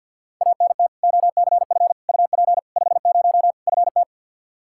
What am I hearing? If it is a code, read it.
Morse code, heard as a keyed tone, about 52 wpm, 700 Hz. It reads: ANT OCF UP 50 FT
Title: Practice QSO - Exchange 2 - Info - 50wpm